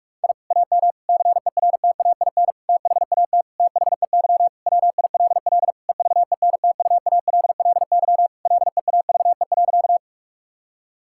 Morse code, heard as a keyed tone, about 41 wpm, 700 Hz. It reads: I AM CERTAIN THAT THEY WILL EVENTUALLY LEAVE.
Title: Sentences from Top 1000 Words - Review Part 6 - 45wpm